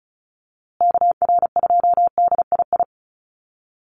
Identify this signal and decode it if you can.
Morse code, heard as a keyed tone, about 34 wpm, 700 Hz. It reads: KR2DII
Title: USA Call Signs Repeated 2x 35wpm